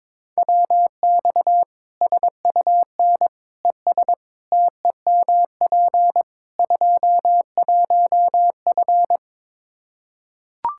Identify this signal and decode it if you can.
Morse code, heard as a keyed tone, about 21 wpm, 700 Hz. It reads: WX SUN ES TEMP 21F
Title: Practice QSO - Exchange 2 - Info - 22wpm